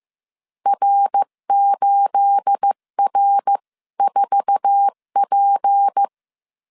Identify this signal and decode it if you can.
Morse code, heard as a keyed tone, about 15 wpm, 800 Hz. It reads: R8R4P
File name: CW / DET / 1 9 1500C S B 14wpm 3302